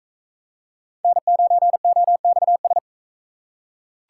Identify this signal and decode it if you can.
Morse code, heard as a keyed tone, about 40 wpm, 700 Hz. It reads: N9OXS